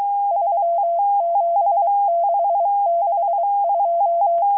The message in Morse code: 37662